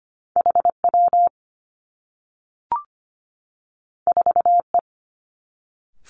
Morse code, heard as a keyed tone, about 25 wpm, 700 Hz. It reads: H W 4 E
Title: Two Challenging Letter-Number Combinations 25wpm